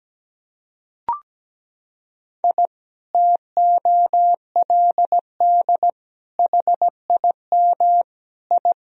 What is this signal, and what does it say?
Morse code, heard as a keyed tone, about 17 wpm, 700 Hz. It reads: I TOLD HIM I
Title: Sentences from Top 1000 Words - Review Part 1 - 17wpm